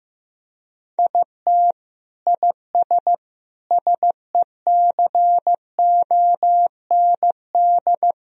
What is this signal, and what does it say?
Morse code, heard as a keyed tone, about 16 wpm, 700 Hz. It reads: IT IS SECOND
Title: Sentences from Top 1000 Words - Review Part 1 - 15wpm